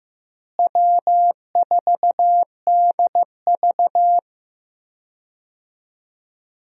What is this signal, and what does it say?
Morse code, heard as a keyed tone, about 15 wpm, 700 Hz. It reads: W4DV
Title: USA Call Signs - 4 Characters - Repeated 2x - 15wpm